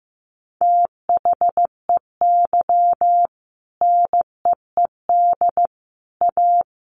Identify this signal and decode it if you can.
Morse code, heard as a keyed tone, about 15 wpm, 700 Hz. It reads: THEY NEED A
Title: Sentences from Top 500 Words 15wpm